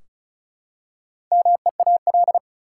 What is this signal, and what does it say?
Morse code, heard as a keyed tone, about 36 wpm, 700 Hz. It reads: MEAL
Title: Four Letter Words - 35wpm